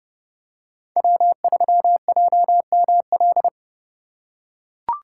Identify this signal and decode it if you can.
Morse code, heard as a keyed tone, about 28 wpm, 700 Hz. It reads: W3JML